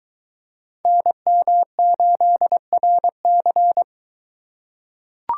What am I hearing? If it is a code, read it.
Morse code, heard as a keyed tone, about 22 wpm, 700 Hz. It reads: NM8RC